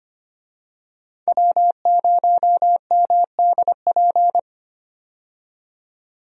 Morse code, heard as a keyed tone, about 25 wpm, 700 Hz. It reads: W0MDP
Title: USA Call Signs - 5 Characters - 25wpm